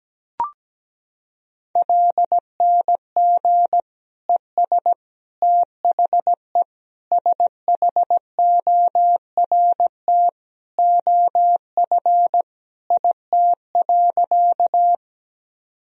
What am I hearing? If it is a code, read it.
Morse code, heard as a keyed tone, about 17 wpm, 700 Hz. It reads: LNG ES THE SHORT OF IT.
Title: Sentences from Top 400 Words and Abbreviations 17wpm